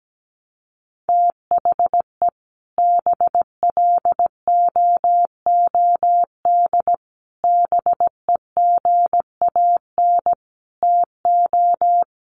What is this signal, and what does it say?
Morse code, heard as a keyed tone, about 17 wpm, 700 Hz. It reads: THE BLOOD BEGAN TO
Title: Sentences from Top 1200 Words 17wpm